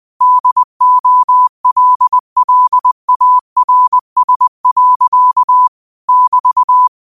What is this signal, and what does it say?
Morse code, heard as a keyed tone, about 20 wpm, 1000 Hz. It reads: DOLLARS. <BT>
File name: qotdfiles / 20wpm